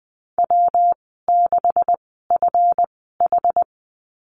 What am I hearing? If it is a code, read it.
Morse code, heard as a keyed tone, about 20 wpm, spaced as for 14 wpm, 700 Hz. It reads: W6FH
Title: 161 Lesson 26g - Letter g - Call Signs 20wpm